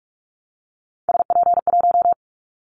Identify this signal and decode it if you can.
Morse code, heard as a keyed tone, about 46 wpm, 700 Hz. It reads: SP1
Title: Callsign Prefixes for Top 25 DXCC - 45wpm